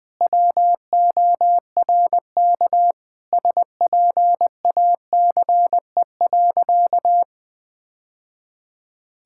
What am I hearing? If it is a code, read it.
Morse code, heard as a keyed tone, about 21 wpm, 700 Hz. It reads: WORK SPACE.
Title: Sentences from Top 1000 Words - Review Part 5 - 20wpm